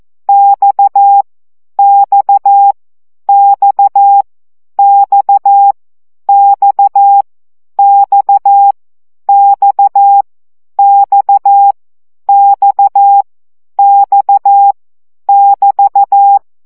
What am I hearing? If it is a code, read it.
Morse code, heard as a keyed tone, about 14 wpm, 800 Hz. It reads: X X X X X X X X X X <BT>
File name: CW gyakorlas x 12 baud